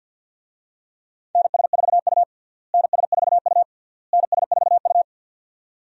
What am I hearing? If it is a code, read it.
Morse code, heard as a keyed tone, about 53 wpm, 700 Hz. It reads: NS4U NS4U NS4U
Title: USA Call Signs - 4 Characters - Repeated 3x - 50wpm